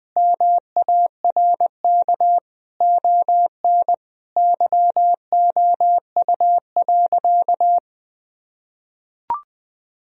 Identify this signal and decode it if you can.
Morse code, heard as a keyed tone, about 20 wpm, 700 Hz. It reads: MARK ON YOU.
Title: Sentences from Top 1000 Words - Review Part 1 - 20wpm